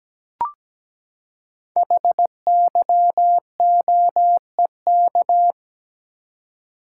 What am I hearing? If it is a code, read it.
Morse code, heard as a keyed tone, about 17 wpm, 700 Hz. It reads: HYOEK
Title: Code Groups - 5 Letters - 17wpm